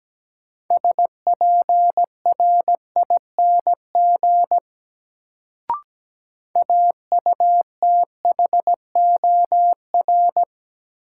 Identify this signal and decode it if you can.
Morse code, heard as a keyed tone, about 17 wpm, 700 Hz. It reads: SPRING AUTHOR
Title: Sentences from Top 1200 Words - New Word Drill - 17wpm